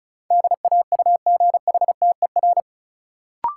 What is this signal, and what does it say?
Morse code, heard as a keyed tone, about 36 wpm, 700 Hz. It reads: DAUGHTER
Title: Sentences from Top 1500 Words - New Word Drill - 35wpm